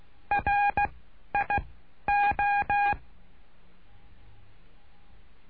Morse code, heard as a keyed tone, about 16 wpm, 810 Hz. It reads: R I O
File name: cursos / CURSO DE TELEGRAFIA / Telegrafia - Trilha 09